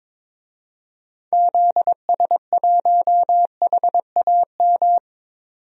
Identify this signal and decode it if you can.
Morse code, heard as a keyed tone, about 21 wpm, 700 Hz. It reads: ZS1HAM
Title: DXCC-100 Call Signs - 6 Characters - 22wpm